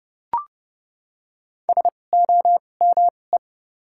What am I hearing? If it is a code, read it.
Morse code, heard as a keyed tone, about 30 wpm, spaced as for 22 wpm, 700 Hz. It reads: SOME
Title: 2x Character Spacing - Top 200 Words - 30wpm